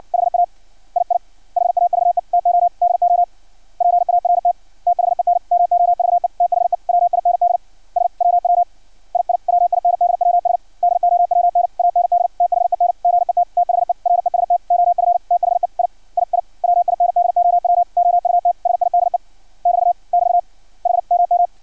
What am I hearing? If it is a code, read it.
Morse code, heard as a keyed tone, about 56 wpm, 700 Hz. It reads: HA II HAVE TO DO PART THEN MOVE THE WINDS WW II WINDWS DOWN AND THEN GET THE REST OF THE III WINDOW OUTSIDE <BT> <BT> HMM
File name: archived / CRSnet